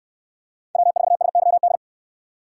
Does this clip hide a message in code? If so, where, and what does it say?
Yes: Morse code from 0.8 to 1.8 s, heard as a keyed tone, about 68 wpm, 700 Hz. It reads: W4SYD